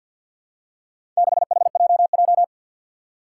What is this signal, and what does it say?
Morse code, heard as a keyed tone, about 49 wpm, 700 Hz. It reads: 6HJJ
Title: Four Challenging Letter-Number Combinations 50wpm